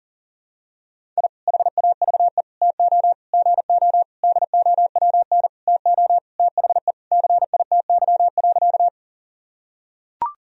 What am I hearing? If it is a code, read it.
Morse code, heard as a keyed tone, about 40 wpm, 700 Hz. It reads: I HAVE TO GO DOWN TO THE CITY.
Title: Sentences from Top 1000 Words - Review Part 1 - 40wpm